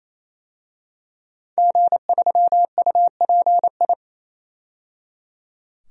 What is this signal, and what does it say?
Morse code, heard as a keyed tone, about 28 wpm, 700 Hz. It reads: G3UPI
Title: DXCC-100 Call Signs - 5 Characters - 28wpm